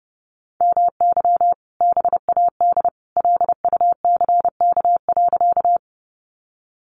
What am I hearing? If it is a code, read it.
Morse code, heard as a keyed tone, about 30 wpm, 700 Hz. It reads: MY BAD LUCK.
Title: Sentences from Top 1500 Words 30wpm